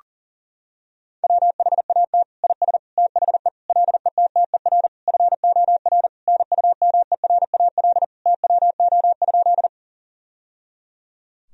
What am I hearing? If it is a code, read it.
Morse code, heard as a keyed tone, about 40 wpm, 700 Hz. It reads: WHAT IS THE LETTER FOR NUMERAL TWO?
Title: Sentences from Top 1000 Words - Review Part 2 - 40wpm